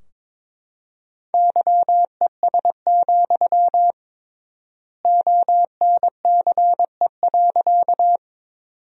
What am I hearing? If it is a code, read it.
Morse code, heard as a keyed tone, about 22 wpm, 700 Hz. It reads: YES, ONCE.
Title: 3x Word-Spacing - Sentences from Top 500 Words - 22wpm